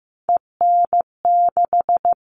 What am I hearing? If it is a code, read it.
Morse code, heard as a keyed tone, about 15 wpm, 700 Hz. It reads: EN6
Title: All Cut Numbers - 3-Digit 15wpm